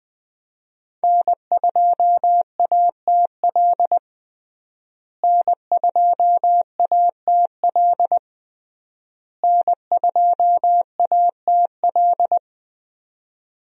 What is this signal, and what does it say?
Morse code, heard as a keyed tone, about 20 wpm, 700 Hz. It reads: N2ATL N2ATL N2ATL
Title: USA Call Signs - 5 Characters - Repeated 3x - 20wpm